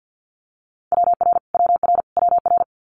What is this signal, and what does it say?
Morse code, heard as a keyed tone, about 50 wpm, 700 Hz. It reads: WR WR WR
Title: Multi-letter Phonemes Repeated 3x 50wpm